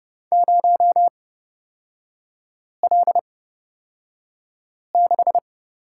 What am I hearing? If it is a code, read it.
Morse code, heard as a keyed tone, about 29 wpm, spaced as for 5 wpm, 700 Hz. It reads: 0L6
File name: Lesson 38e - Letter q - Three Characters 30wpm